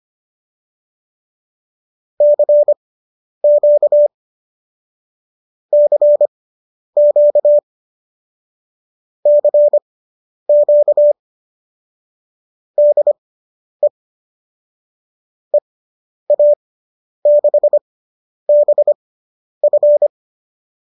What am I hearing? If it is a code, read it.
Morse code, heard as a keyed tone, about 25 wpm, spaced as for 10 wpm, 590 Hz. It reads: CQ CQ CQ DE EA6BF